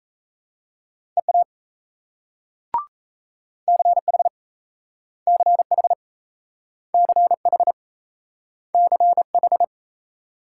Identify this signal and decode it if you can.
Morse code, heard as a keyed tone, about 38 wpm, 700 Hz. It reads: EA CH CH CH CH